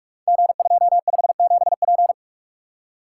Morse code, heard as a keyed tone, about 44 wpm, 690 Hz. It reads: G257P
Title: Code Groups - 5 Letter:Numbers - 45wpm